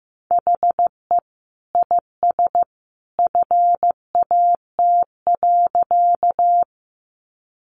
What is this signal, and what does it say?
Morse code, heard as a keyed tone, about 15 wpm, 700 Hz. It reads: HE IS FAT.
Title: Sentences from Top 600 Words 15wpm